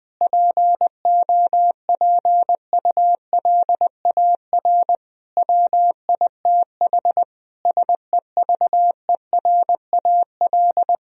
Morse code, heard as a keyed tone, about 20 wpm, 700 Hz. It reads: POPULAR WITH SEVERAL
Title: Sentences from Top 1000 Words - Review Part 3 - 20wpm